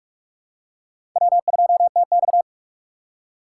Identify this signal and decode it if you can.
Morse code, heard as a keyed tone, about 46 wpm, 700 Hz. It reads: W2TX
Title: USA Call Signs - 4 Characters - 3x Character Spacing - 45wpm